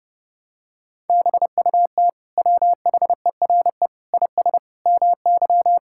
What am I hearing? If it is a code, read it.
Morse code, heard as a keyed tone, about 30 wpm, 700 Hz. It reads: BUT WHERE IS MY
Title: Sentences from Top 1600 Words 30wpm